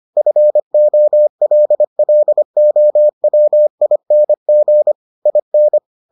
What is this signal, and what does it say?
Morse code, heard as a keyed tone, about 25 wpm, 600 Hz. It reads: FOLLOWING IN